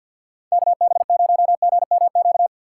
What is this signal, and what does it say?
Morse code, heard as a keyed tone, about 52 wpm, 700 Hz. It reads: KB0GMQ
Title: USA Call Signs - 6 Characters - 3x Character Spacing - 50wpm